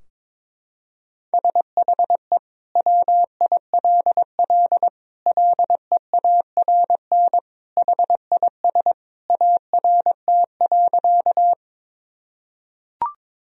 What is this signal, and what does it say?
Morse code, heard as a keyed tone, about 22 wpm, 700 Hz. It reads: SHE WILL LEARN HIS ART.
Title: Sentences from Top 1000 Words - Review Part 3 - 22wpm